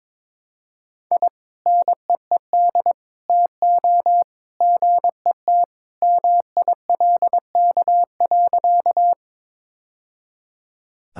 Morse code, heard as a keyed tone, about 22 wpm, 700 Hz. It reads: I NEED TO GET MILK.
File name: practice / 22wpm / Sentences with 2-Word Phrases - 22wpm